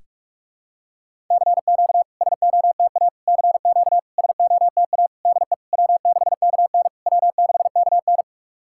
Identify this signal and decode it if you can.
Morse code, heard as a keyed tone, about 44 wpm, 700 Hz. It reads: CQ SOTA CQ SOTA DE W6KN W6KN
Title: SOTA QSO Practice 45wpm